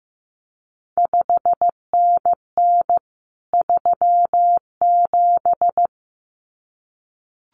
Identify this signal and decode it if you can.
Morse code, heard as a keyed tone, about 15 wpm, 700 Hz. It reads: 5NN 37
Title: CQ WW DX Contest Exchange Encore - 15wpm